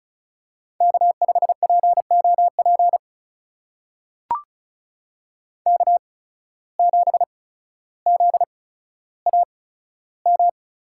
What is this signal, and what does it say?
Morse code, heard as a keyed tone, about 34 wpm, spaced as for 15 wpm, 700 Hz. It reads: K5POP K7ZAM
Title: USA Call Signs - 5 Characters - 8x Character Spacing - 35wpm